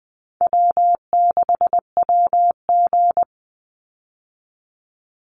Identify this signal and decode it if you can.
Morse code, heard as a keyed tone, about 20 wpm, 700 Hz. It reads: W6WG
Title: Easy Call Signs Repeated 3x - 20wpm